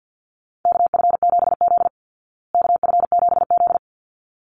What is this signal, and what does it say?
Morse code, heard as a keyed tone, about 50 wpm, 700 Hz. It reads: KF7Z KF7Z
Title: Call Signs Repeated 3x - 50wpm